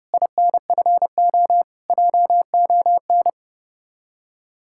Morse code, heard as a keyed tone, about 30 wpm, 700 Hz. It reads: INFO JON
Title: Speed-Racing Common QSO Phrases - 30wpm